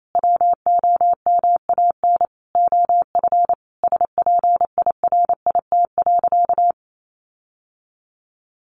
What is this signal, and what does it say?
Morse code, heard as a keyed tone, about 28 wpm, 700 Hz. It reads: WOMAN OF SPIRIT.
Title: Sentences from Top 1400 Words 28wpm